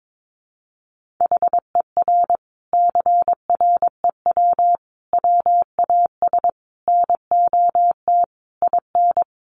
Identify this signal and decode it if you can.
Morse code, heard as a keyed tone, about 22 wpm, 700 Hz. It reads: HER CREW WAS NOT IN
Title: Sentences from Top 1500 Words 22wpm